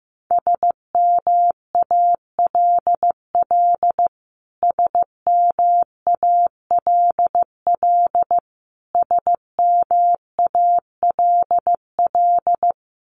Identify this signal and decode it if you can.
Morse code, heard as a keyed tone, about 15 wpm, 700 Hz. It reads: SMALL SMALL SMALL
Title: Top 300 Words Repeated 15wpm